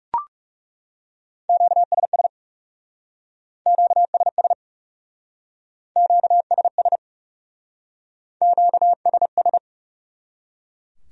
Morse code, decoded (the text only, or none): QSS QSS QSS QSS